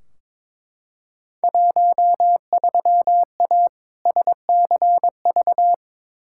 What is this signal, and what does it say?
Morse code, heard as a keyed tone, about 22 wpm, 700 Hz. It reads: 13A SCV
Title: Field Day Exchanges Encore - 22wpm